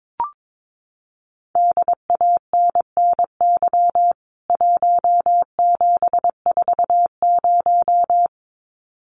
Morse code, heard as a keyed tone, about 22 wpm, 700 Hz. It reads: DANNY 1740
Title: CWT Exchange 22wpm